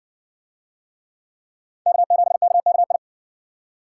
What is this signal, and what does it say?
Morse code, heard as a keyed tone, about 57 wpm, 700 Hz. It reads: K7KKI